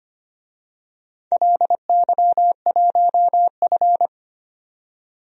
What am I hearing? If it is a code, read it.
Morse code, heard as a keyed tone, about 25 wpm, 700 Hz. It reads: LY1F
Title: DXCC-100 Call Signs - 4 Characters - 25wpm